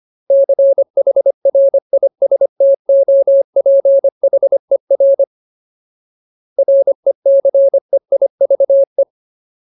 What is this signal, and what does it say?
Morse code, heard as a keyed tone, about 25 wpm, 550 Hz. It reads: CHRISTOPHER RECEIVE
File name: Ultimate Word List / 25WPM Ultimate